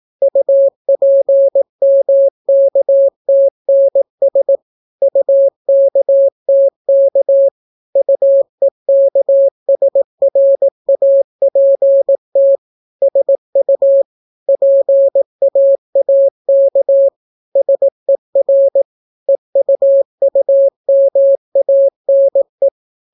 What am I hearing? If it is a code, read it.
Morse code, heard as a keyed tone, about 18 wpm, 550 Hz. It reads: UPMKTNS UKTK UEKSRAPT SU PAAK SER EUUMANE